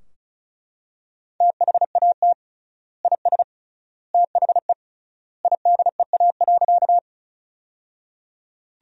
Morse code, heard as a keyed tone, about 36 wpm, 700 Hz. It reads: THAT IS THE IDEA.
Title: 3x Word-Spacing - Sentences from Top 300 Words - 35wpm